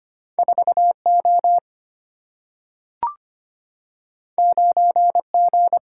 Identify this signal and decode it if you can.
Morse code, heard as a keyed tone, about 25 wpm, 700 Hz. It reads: 4O 9G
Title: Single-Letter Single-Number 25wpm